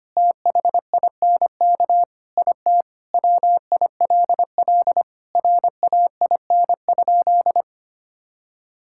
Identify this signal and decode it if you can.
Morse code, heard as a keyed tone, about 25 wpm, 700 Hz. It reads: THINK IT WILL RAIN?
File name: practice / 25wpm / Sentences with 4-Word Phrases - 25wpm